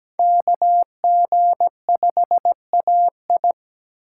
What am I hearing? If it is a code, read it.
Morse code, heard as a keyed tone, about 20 wpm, 700 Hz. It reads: KG5AI
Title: Speed Racing - USA 2x2 Call Signs 17wpm